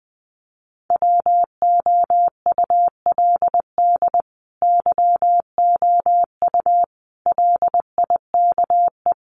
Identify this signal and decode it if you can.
Morse code, heard as a keyed tone, about 20 wpm, 700 Hz. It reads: WOULD YOU LIKE
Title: Sentences from Top 1000 Words 20wpm